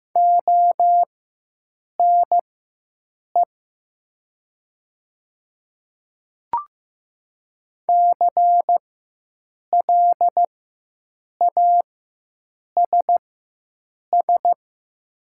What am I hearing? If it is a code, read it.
Morse code, heard as a keyed tone, about 15 wpm, spaced as for 7 wpm, 700 Hz. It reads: ONE CLASS
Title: 4x Character Spacing - Top 500 Words - 15wpm